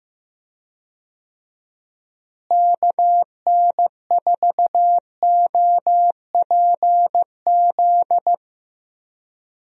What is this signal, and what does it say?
Morse code, heard as a keyed tone, about 15 wpm, 700 Hz. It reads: KN4OPZ
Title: USA Call Signs - 6 Characters - Repeated 2x - 15wpm